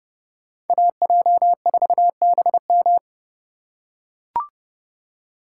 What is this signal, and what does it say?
Morse code, heard as a keyed tone, about 30 wpm, 700 Hz. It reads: AJ4BM
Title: USA Call Signs - 5 Characters - 4x Character Spacing - 30wpm